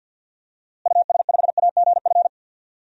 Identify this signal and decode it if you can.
Morse code, heard as a keyed tone, about 51 wpm, 700 Hz. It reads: US5AGF
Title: DXCC-100 Call Signs - 6 Characters - Repeated 2x - 50wpm